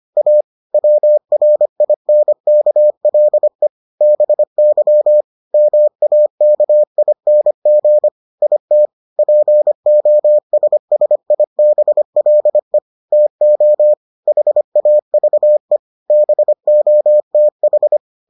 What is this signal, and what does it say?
Morse code, heard as a keyed tone, about 25 wpm, 600 Hz. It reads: A WRINKLE BY MAKING IT POSSIBLE TO HAVE BOTH